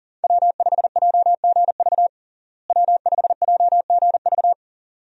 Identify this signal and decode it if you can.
Morse code, heard as a keyed tone, about 40 wpm, 700 Hz. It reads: W5JGV W5JGV
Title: USA Call Signs - 5 Characters - Repeated 2x - 40wpm